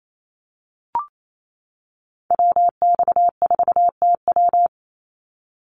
Morse code, heard as a keyed tone, about 28 wpm, 700 Hz. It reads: WX4TW
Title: NA SOTA Activator Callsigns 28wpm